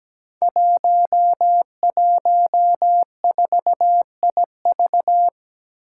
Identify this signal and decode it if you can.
Morse code, heard as a keyed tone, about 17 wpm, 700 Hz. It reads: 114IV
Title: Five Challenging Letter-Number Combinations 17wpm